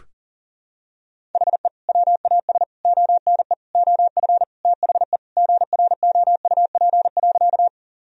Morse code, heard as a keyed tone, about 40 wpm, 700 Hz. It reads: HE WAS ONE OF THE GROUP.
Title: Sentences from Top 1000 Words - Review Part 1 - 40wpm